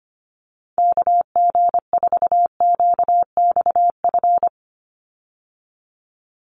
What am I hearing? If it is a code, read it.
Morse code, heard as a keyed tone, about 25 wpm, 700 Hz. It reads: KG4QXF
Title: USA Call Signs 25wpm